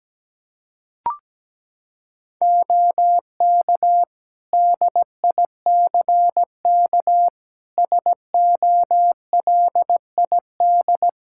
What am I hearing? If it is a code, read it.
Morse code, heard as a keyed tone, about 17 wpm, 700 Hz. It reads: OK DICK SOLID
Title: Practice QSO - Exchange 3 - More Info - 17wpm